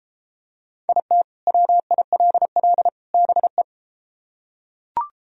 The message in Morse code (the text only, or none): IT WILL BE